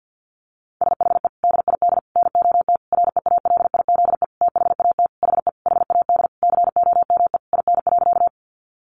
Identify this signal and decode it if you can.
Morse code, heard as a keyed tone, about 48 wpm, 700 Hz. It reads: SHE DID NOT REALIZE THAT HE HAD COME IN.
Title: Sentences from Top 1000 Words 50wpm